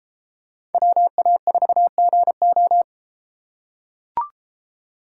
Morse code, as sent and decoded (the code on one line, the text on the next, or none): .-- .- ....- --. ---
WA4GO